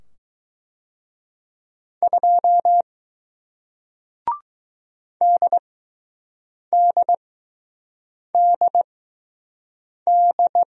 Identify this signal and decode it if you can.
Morse code, heard as a keyed tone, about 20 wpm, spaced as for 6 wpm, 700 Hz. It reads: 2 DDDD